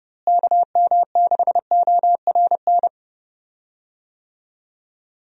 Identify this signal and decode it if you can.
Morse code, heard as a keyed tone, about 30 wpm, 700 Hz. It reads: KM6ORN
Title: USA Call Signs - 6 Characters - Repeated 2x - 30wpm